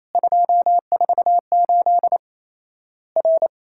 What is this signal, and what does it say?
Morse code, heard as a keyed tone, about 28 wpm, 700 Hz. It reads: 248 R
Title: Worked All Europe DX Contest - QTC Exchange - 28wpm